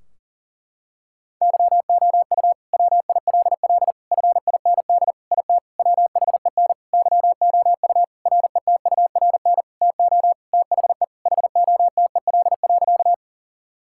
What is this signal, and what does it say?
Morse code, heard as a keyed tone, about 40 wpm, 700 Hz. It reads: YOU WILL FIND IT WHEN YOU RETURN TO THE HOTEL.
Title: Sentences from Top 1600 Words 40wpm